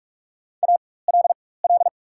Morse code, heard as a keyed tone, about 44 wpm, spaced as for 21 wpm, 700 Hz. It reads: ALL
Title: 4x Character Spacing - Top 200 Words - 45wpm